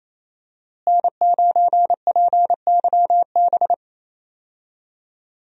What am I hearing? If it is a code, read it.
Morse code, heard as a keyed tone, about 27 wpm, 700 Hz. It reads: N9PYB
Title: USA Call Signs - 5 Characters - Repeated 3x - 28wpm